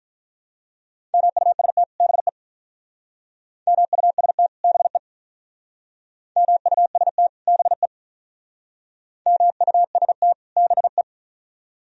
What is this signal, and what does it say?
Morse code, heard as a keyed tone, about 46 wpm, 700 Hz. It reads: MUST BE MUST BE MUST BE MUST BE